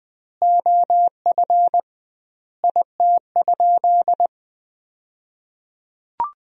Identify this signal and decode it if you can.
Morse code, heard as a keyed tone, about 21 wpm, 700 Hz. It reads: OF IT?
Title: 2x Word-Spacing - Sentences from Top 100 Words - 20wpm